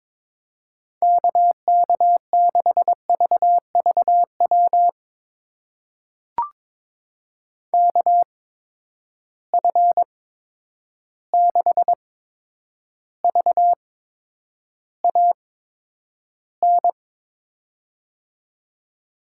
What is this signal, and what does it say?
Morse code, heard as a keyed tone, about 22 wpm, spaced as for 9 wpm, 700 Hz. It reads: KK6VVW KF6VAN
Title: USA Call Signs - 6 Characters - 8x Character Spacing - 22wpm